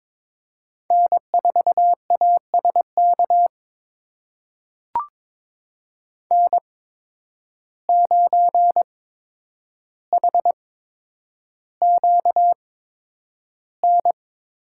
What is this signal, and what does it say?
Morse code, heard as a keyed tone, about 22 wpm, spaced as for 9 wpm, 700 Hz. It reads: N4ASK N9HQN
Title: USA Call Signs - 5 Characters - 8x Character Spacing - 22wpm